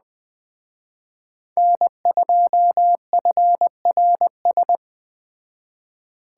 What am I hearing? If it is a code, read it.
Morse code, heard as a keyed tone, about 21 wpm, 700 Hz. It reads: N2FRS
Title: Speed Racing - USA 1x3 Call Signs 17wpm